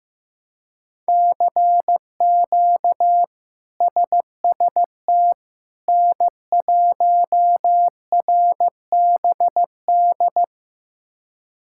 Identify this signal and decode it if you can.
Morse code, heard as a keyed tone, about 15 wpm, 700 Hz. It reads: CQ SST N1RBD
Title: K1USN SST QSO Practice - 15wpm